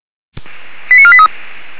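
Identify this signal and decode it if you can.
Five-tone selective call (ZVEI), heard as quick runs of five tones, tones of about 70 ms. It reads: 89483
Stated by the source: ZVEI five-tone call 89123 received on SDR.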